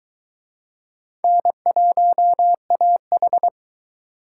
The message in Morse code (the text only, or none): N1AH